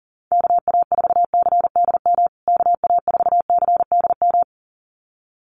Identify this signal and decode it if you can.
Morse code, heard as a keyed tone, about 40 wpm, 700 Hz. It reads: KA4CDM KA4CDM
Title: USA Call Signs Repeated 2x 40wpm